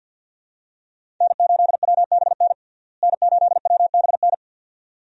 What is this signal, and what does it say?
Morse code, heard as a keyed tone, about 53 wpm, 700 Hz. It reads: N8WBN N8WBN
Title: USA Call Signs - 5 Characters - Repeated 2x - 50wpm